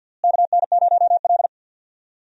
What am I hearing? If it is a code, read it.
Morse code, heard as a keyed tone, about 47 wpm, 700 Hz. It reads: KN0L
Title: USA Call Signs - 4 Characters - Repeated 2x - 50wpm